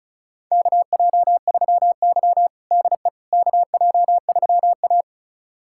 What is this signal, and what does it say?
Morse code, heard as a keyed tone, about 34 wpm, 700 Hz. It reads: KJ3Y DE KJ3A
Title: USA Callsign DE Callsign - 4 Characters - 35wpm